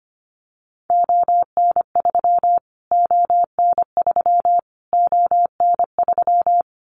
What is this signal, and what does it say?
Morse code, heard as a keyed tone, about 25 wpm, 700 Hz. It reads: ON3 ON3 ON3
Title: Callsign Prefixes for Top 25 DXCC - Repeated 3x - 25wpm